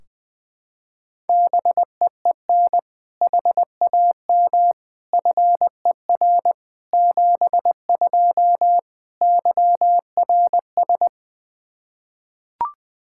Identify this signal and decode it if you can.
Morse code, heard as a keyed tone, about 20 wpm, 700 Hz. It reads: BEEN HAM FER 72 YRS
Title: Practice QSO - Exchange 3 - More Info - 20wpm